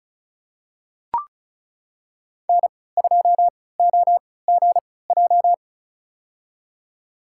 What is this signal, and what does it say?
Morse code, heard as a keyed tone, about 34 wpm, spaced as for 20 wpm, 700 Hz. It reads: N2OGJ
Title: USA Call Signs - 5 Characters - 3x Character Spacing - 35wpm